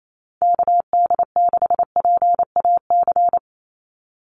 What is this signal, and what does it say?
Morse code, heard as a keyed tone, about 28 wpm, 700 Hz. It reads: KD6PAC
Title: USA Call Signs 28wpm